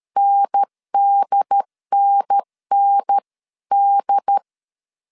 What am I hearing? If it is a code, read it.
Morse code, heard as a keyed tone, about 13 wpm, 800 Hz. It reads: NDNND
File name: CW / DET / 1 3 1500C S A 12wpm 2905